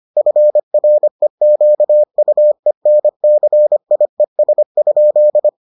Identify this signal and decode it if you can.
Morse code, heard as a keyed tone, about 25 wpm, 600 Hz. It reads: FREQUENCIES?